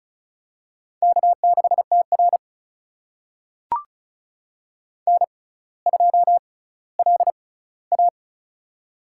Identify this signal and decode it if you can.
Morse code, heard as a keyed tone, about 36 wpm, spaced as for 18 wpm, 700 Hz. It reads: K6TR N2LA
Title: USA Call Signs - 4 Characters - 6x Character Spacing - 35wpm